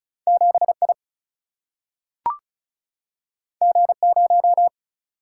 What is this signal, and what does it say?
Morse code, heard as a keyed tone, about 35 wpm, 700 Hz. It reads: 7I G0
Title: Single-Letter Single-Number 35wpm